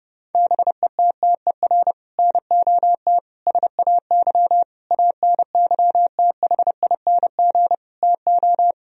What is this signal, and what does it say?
Morse code, heard as a keyed tone, about 30 wpm, 700 Hz. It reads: BETTER NOT SAY ANYTHING TO
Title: Sentences from Top 1000 Words - Review Part 4 - 30wpm